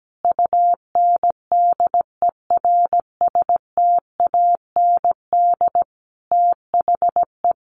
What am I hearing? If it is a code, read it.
Morse code, heard as a keyed tone, about 17 wpm, 700 Hz. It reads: UNDERSTAND THE
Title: Sentences from Top 1200 Words 17wpm